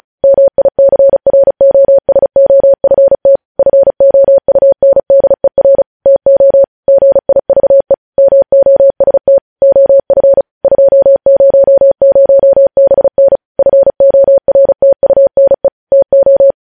Morse code, heard as a keyed tone, about 36 wpm, 550 Hz. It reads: MICROSOFT FOUNDER TO GIVE MOST OF 200BN FORTUNE TO
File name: podcast / News headlines-1x-35x35